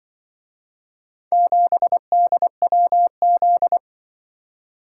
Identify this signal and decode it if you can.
Morse code, heard as a keyed tone, about 24 wpm, 700 Hz. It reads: 7DWZ
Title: Speed-Racing - Four Letter-Number Combinations - 20wpm